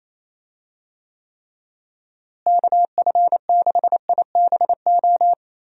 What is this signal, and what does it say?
Morse code, heard as a keyed tone, about 27 wpm, 700 Hz. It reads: KF6IBO
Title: USA Call Signs - 6 Characters - Repeated 2x - 28wpm